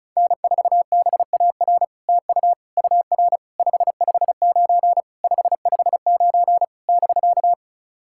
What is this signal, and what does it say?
Morse code, heard as a keyed tone, about 36 wpm, 700 Hz. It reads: N4BAR TU UR 559 559 <BK>
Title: SOTA QSO Practice 35wpm